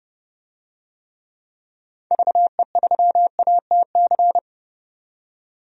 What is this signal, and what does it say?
Morse code, heard as a keyed tone, about 30 wpm, 700 Hz. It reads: VE3ATC
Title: DXCC-100 Call Signs - 6 Characters - 30wpm